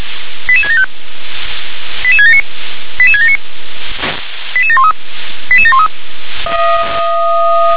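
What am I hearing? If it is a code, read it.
Five-tone selective call (ZVEI), heard as quick runs of five tones, tones of about 70 ms. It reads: 88675 88679 88679 88713 88713
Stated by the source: Five-tone ZVEI selective call 88774, received by SDR.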